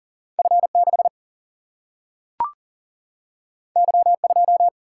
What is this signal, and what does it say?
Morse code, heard as a keyed tone, about 40 wpm, 700 Hz. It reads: F6 Y2
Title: Single-Letter Single-Number 40wpm